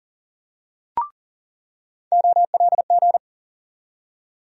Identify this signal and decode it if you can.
Morse code, heard as a keyed tone, about 40 wpm, 700 Hz. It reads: OLG
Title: USA Call Sign Postfixes - 3 Letters 40wpm